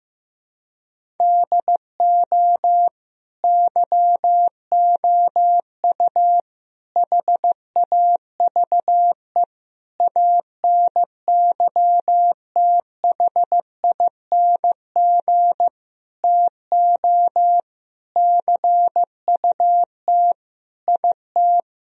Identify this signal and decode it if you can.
Morse code, heard as a keyed tone, about 15 wpm, 700 Hz. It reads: DO YOU HAVE ANYTHING TO CUT IT
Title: Sentences from Top 1000 Words - Review Part 1 - 15wpm